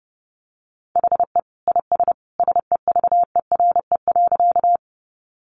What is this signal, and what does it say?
Morse code, heard as a keyed tone, about 28 wpm, 700 Hz. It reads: HE IS SEVERE.
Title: Sentences from Top 1400 Words 30wpm